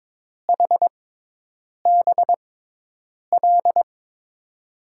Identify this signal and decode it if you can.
Morse code, heard as a keyed tone, about 23 wpm, spaced as for 8 wpm, 700 Hz. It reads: HBL
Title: USA Call Signs - 6 Characters - 6x Character Spacing - 22wpm